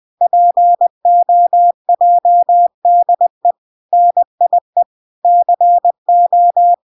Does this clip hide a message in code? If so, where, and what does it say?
Yes: Morse code from 0.2 to 6.7 s, heard as a keyed tone, about 20 wpm, 700 Hz. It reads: POJDE NIE CO